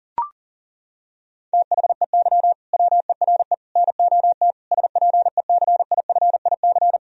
Morse code, heard as a keyed tone, about 40 wpm, 700 Hz. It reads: THEY WERE NOT SPECIFIC
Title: Sentences from Top 1000 Words - Review Part 3 - 40wpm